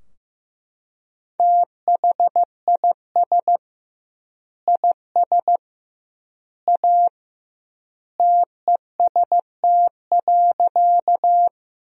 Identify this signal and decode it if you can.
Morse code, heard as a keyed tone, about 15 wpm, 700 Hz. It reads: THIS IS A TEST.
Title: 2x Word-Spacing - Sentences from Top 500 Words - 15wpm